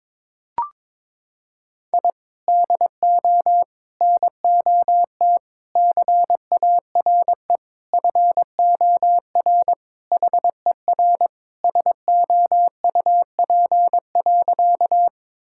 Morse code, heard as a keyed tone, about 22 wpm, 700 Hz. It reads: I DO NOT CARE FOR HER SOUP.
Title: Sentences from Top 1000 Words - Review Part 6 - 22wpm